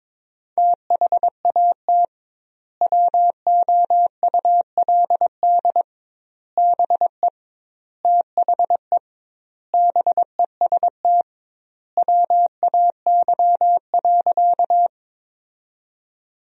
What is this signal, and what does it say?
Morse code, heard as a keyed tone, about 23 wpm, 700 Hz. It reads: THAT WOULD BE THE BEST WAY.
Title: 2x Word-Spacing - Sentences from Top 400 Words - 22wpm